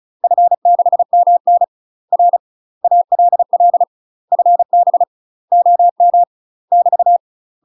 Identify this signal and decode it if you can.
Morse code, heard as a keyed tone, about 34 wpm, 700 Hz. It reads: F6MN R ALL FB OM <BT>
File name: UK MASTER QSOs TRAINING / 35WPM / UK MASTER QSOs 0031